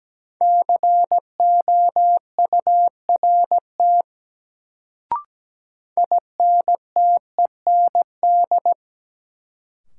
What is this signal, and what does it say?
Morse code, heard as a keyed tone, about 17 wpm, 700 Hz. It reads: COURT INTEND
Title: Sentences from Top 1600 Words - New Word Drill - 17wpm